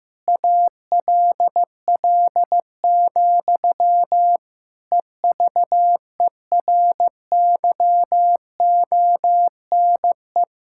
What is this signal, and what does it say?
Morse code, heard as a keyed tone, about 15 wpm, 700 Hz. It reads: ALL, EVERYONE
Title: Sentences from Top 1000 Words - Review Part 5 - 15wpm